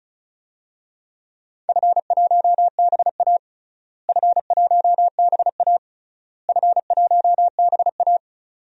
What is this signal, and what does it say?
Morse code, heard as a keyed tone, about 36 wpm, 700 Hz. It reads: F1BA F1BA F1BA
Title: DXCC-100 Call Signs - 4 Characters - Repeated 3x - 35wpm